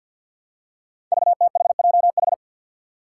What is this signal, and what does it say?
Morse code, heard as a keyed tone, about 48 wpm, 700 Hz. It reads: VTHJH
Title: Five Challenging Letter-Number Combinations 50wpm